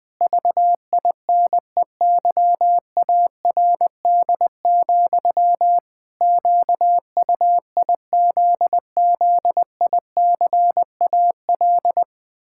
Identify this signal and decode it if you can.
Morse code, heard as a keyed tone, about 20 wpm, 700 Hz. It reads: VINEYARD, QUIZZICAL
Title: Pangram Challenge v2 - 20wpm